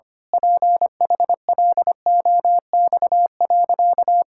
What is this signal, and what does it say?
Morse code, heard as a keyed tone, about 25 wpm, 700 Hz. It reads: PHLOX.
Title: Pangram Challenge v2 - 25wpm